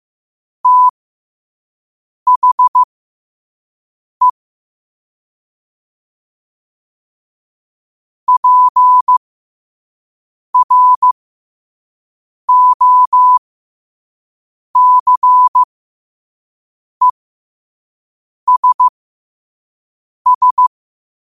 Morse code, heard as a keyed tone, about 15 wpm, spaced as for 5 wpm, 1000 Hz. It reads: THE PROCESS
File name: qotdfiles / 5wpm